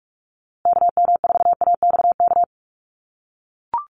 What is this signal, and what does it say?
Morse code, heard as a keyed tone, about 44 wpm, 700 Hz. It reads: KM4AXK
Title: USA Call Signs Repeated 3x 45wpm